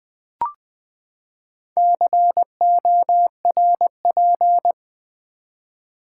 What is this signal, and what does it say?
Morse code, heard as a keyed tone, about 20 wpm, 700 Hz. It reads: CORP
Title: Build A Word - 4 Letters At A Time - 11-Letter Words - 20wpm